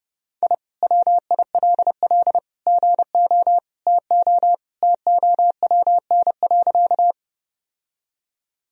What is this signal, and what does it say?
Morse code, heard as a keyed tone, about 30 wpm, 700 Hz. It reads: I WILL GO TO TOWN.
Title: Sentences from Top 1000 Words - Review Part 2 - 30wpm